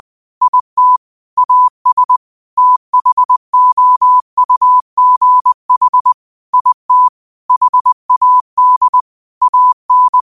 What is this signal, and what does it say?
Morse code, heard as a keyed tone, about 20 wpm, 1000 Hz. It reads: IT AS THOUGH IT HAD AN
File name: qotdfiles / 20wpm